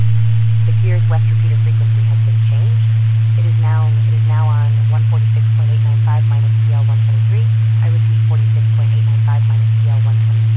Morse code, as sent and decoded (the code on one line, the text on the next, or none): none